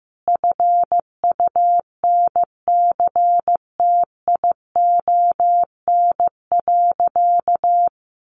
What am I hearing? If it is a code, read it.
Morse code, heard as a keyed tone, about 15 wpm, 700 Hz. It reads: FUNCTION.
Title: Sentences from Top 1100 Words 15wpm